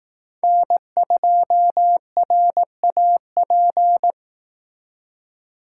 Morse code, heard as a keyed tone, about 18 wpm, 700 Hz. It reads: N2RAP
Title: Speed Racing - USA Call Signs 15wpm